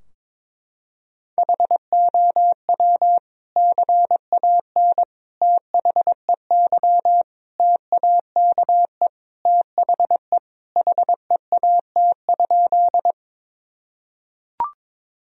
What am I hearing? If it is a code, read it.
Morse code, heard as a keyed tone, about 22 wpm, 700 Hz. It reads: HOW CAN THEY TAKE THE HEAT?
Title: Sentences from Top 1000 Words - Review Part 3 - 22wpm